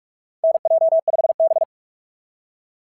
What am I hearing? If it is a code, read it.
Morse code, heard as a keyed tone, about 44 wpm, 660 Hz. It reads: NJ5B
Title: SOTA QSO Practice 45wpm